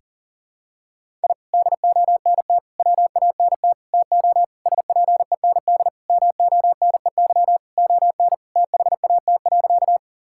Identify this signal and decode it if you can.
Morse code, heard as a keyed tone, about 37 wpm, 700 Hz. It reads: I DONT WANT TO SPEND MONEY ON THAT.
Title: Sentences from Top 1000 Words - Review Part 5 - 40wpm